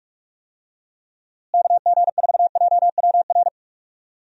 Morse code, heard as a keyed tone, about 46 wpm, 700 Hz. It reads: KG4JWR
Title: USA Call Signs - 6 Characters - Repeated 2x - 45wpm